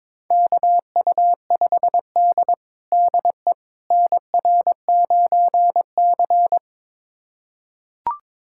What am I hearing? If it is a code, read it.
Morse code, heard as a keyed tone, about 22 wpm, 700 Hz. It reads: KU5D DE NR9C
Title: USA Callsign DE Callsign - 4 Characters - 22wpm